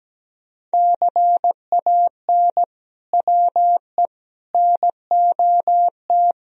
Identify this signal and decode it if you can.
Morse code, heard as a keyed tone, about 17 wpm, 700 Hz. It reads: CAN WE NOT
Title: Sentences from Top 1000 Words - Review Part 4 - 17wpm